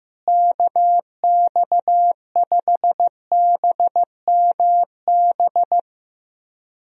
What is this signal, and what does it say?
Morse code, heard as a keyed tone, about 15 wpm, 700 Hz. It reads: KX5BMB
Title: USA Call Signs - 6 Characters - Repeated 2x - 15wpm